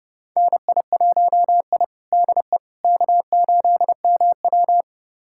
Morse code, heard as a keyed tone, about 30 wpm, 700 Hz. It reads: NI1I DE K8MW
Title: USA Callsign DE Callsign - 4 Characters - 30wpm